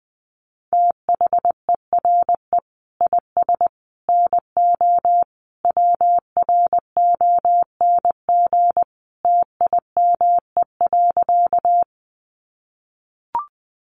Morse code, heard as a keyed tone, about 20 wpm, 700 Hz. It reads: THERE IS NO WRONG TIME.
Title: Sentences from Top 900 Words 20wpm